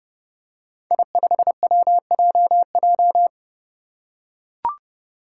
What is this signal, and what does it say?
Morse code, heard as a keyed tone, about 29 wpm, 700 Hz. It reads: I5WJJ
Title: Five Challenging Letter-Number Combinations 30wpm